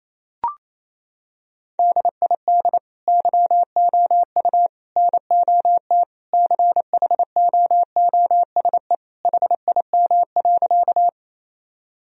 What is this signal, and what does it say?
Morse code, heard as a keyed tone, about 28 wpm, 700 Hz. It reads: DID YOU NOT CHOOSE HIM.
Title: Sentences from Top 1000 Words - Review Part 4 - 28wpm